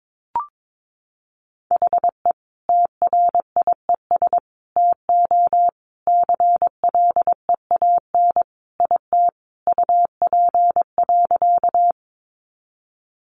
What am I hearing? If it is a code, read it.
Morse code, heard as a keyed tone, about 22 wpm, 700 Hz. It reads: HE TRIES TO CLEAN IT UP.
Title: Additional Easy Phrases 22wpm